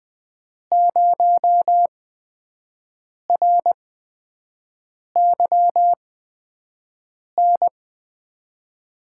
Morse code, heard as a keyed tone, about 20 wpm, spaced as for 5 wpm, 700 Hz. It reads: 0RYN
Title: USA Call Signs - 5 Characters - 8x Character Spacing - 20wpm